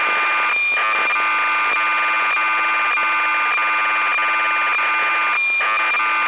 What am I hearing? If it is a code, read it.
Cassette data (Kansas City type).
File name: uo-11-trimmed